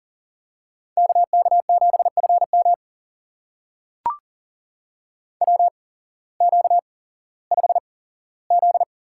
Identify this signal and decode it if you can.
Morse code, heard as a keyed tone, about 40 wpm, spaced as for 19 wpm, 700 Hz. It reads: KK7FM WQ5Z
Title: USA Call Signs - 8x Character Spacing - 40wpm